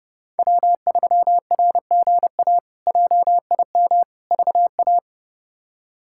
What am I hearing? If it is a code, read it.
Morse code, heard as a keyed tone, about 30 wpm, 700 Hz. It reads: W3RGA JIM VA
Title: K1USN SST QSO Practice - 30wpm